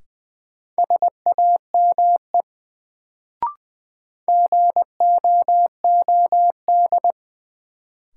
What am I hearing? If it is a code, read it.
Morse code, heard as a keyed tone, about 20 wpm, 700 Hz. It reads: SAME GOOD
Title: Sets of 2 Words - New Word Drill - 20wpm